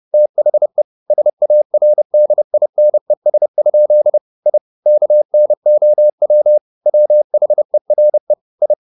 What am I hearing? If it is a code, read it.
Morse code, heard as a keyed tone, about 30 wpm, 600 Hz. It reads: THE SARDINES? I KNOW WHERE I
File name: the old man and the sea / the old man and the sea 1 30wpm